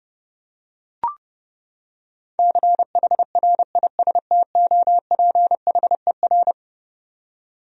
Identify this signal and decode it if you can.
Morse code, heard as a keyed tone, about 30 wpm, 700 Hz. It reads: CHRISTOPHER
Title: USA Common Names - 30wpm